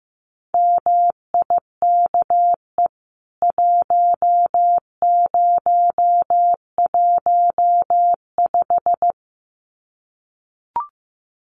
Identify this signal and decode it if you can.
Morse code, heard as a keyed tone, about 15 wpm, 700 Hz. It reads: MIKE 1015
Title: CWT Exchange 15wpm